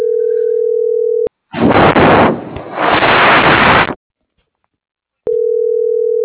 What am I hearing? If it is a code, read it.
Ringback tone (North America), 440 + 480 Hz.